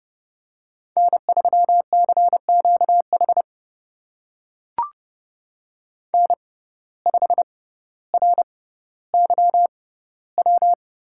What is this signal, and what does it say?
Morse code, heard as a keyed tone, about 32 wpm, spaced as for 16 wpm, 700 Hz. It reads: N3CQH N5RYW
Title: USA Call Signs - 5 Characters - 6x Character Spacing - 30wpm